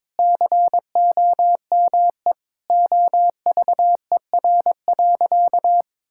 Morse code, heard as a keyed tone, about 21 wpm, 700 Hz. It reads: COME OVER.
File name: Sentences with 4-Word Phrases - 22wpm